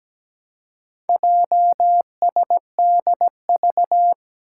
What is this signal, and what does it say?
Morse code, heard as a keyed tone, about 17 wpm, 700 Hz. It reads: JSDV
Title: Four Challenging Letter-Number Combinations 17wpm